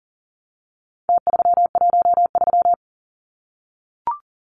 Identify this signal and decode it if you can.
Morse code, heard as a keyed tone, about 40 wpm, 700 Hz. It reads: T313
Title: Numbers - Contest Style - 4-Digit 40wpm